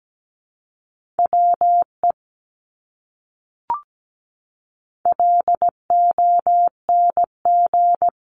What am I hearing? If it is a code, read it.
Morse code, heard as a keyed tone, about 17 wpm, 700 Hz. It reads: WE LONG
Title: Top 200 Words Repeated 17wpm